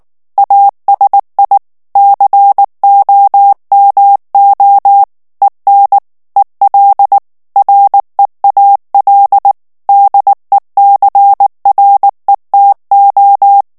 Morse code, heard as a keyed tone, about 19 wpm, 800 Hz. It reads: ASI COMO EN EL REAL DECRETO
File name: morse mp3 / ESPAGNOL TEXTE EN CLAIR 2 VITESSE 18 MOTS MINUTES